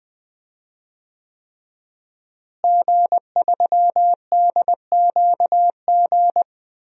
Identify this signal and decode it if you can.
Morse code, heard as a keyed tone, about 20 wpm, 700 Hz. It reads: G3DQG
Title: DXCC-100 Call Signs - 5 Characters - Repeated 3x - 20wpm